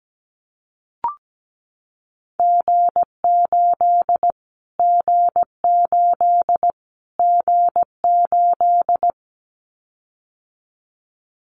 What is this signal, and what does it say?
Morse code, heard as a keyed tone, about 17 wpm, 700 Hz. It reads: G8 G8 G8
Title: Callsign Prefixes for Top 25 DXCC - Repeated 3x - 17wpm